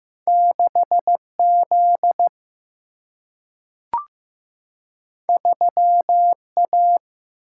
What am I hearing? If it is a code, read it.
Morse code, heard as a keyed tone, about 15 wpm, 700 Hz. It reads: 6Z 3A
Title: Single-Letter Single-Number 15wpm